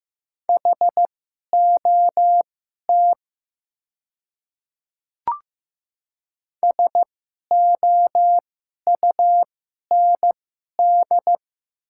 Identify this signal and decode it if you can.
Morse code, heard as a keyed tone, about 15 wpm, spaced as for 11 wpm, 700 Hz. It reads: HOT SOUND
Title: 2x Character Spacing - Top 100 Words - 15wpm